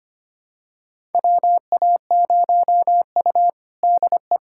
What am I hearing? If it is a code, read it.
Morse code, heard as a keyed tone, about 25 wpm, 700 Hz. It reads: WA0U DE
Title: USA Callsign DE Callsign - 4 Characters - 25wpm